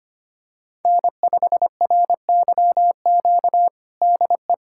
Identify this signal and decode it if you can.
Morse code, heard as a keyed tone, about 25 wpm, 700 Hz. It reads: N5RYQ DE
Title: USA Callsign DE Callsign - 5 Characters - 25wpm